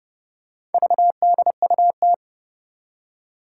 Morse code, heard as a keyed tone, about 29 wpm, 700 Hz. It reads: VDUT
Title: Four Challenging Letter-Number Combinations 30wpm